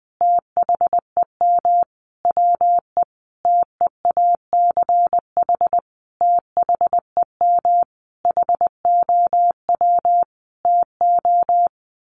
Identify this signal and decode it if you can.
Morse code, heard as a keyed tone, about 20 wpm, 700 Hz. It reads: THEM WE TEACH THEM HOW TO
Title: Sentences from Top 1300 Words 20wpm